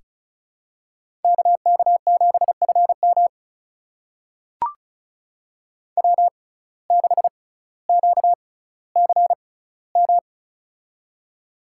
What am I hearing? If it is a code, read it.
Morse code, heard as a keyed tone, about 36 wpm, spaced as for 18 wpm, 700 Hz. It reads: KK7FM W6QCM
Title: USA Call Signs - 5 Characters - 6x Character Spacing - 35wpm